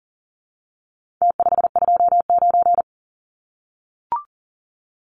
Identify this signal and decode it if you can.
Morse code, heard as a keyed tone, about 40 wpm, 700 Hz. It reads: T529
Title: Numbers - Contest Style - 4-Digit 40wpm